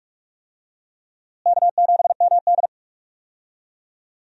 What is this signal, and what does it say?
Morse code, heard as a keyed tone, about 45 wpm, 700 Hz. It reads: K7MD
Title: USA Call Signs - 4 Characters - Repeated 2x - 45wpm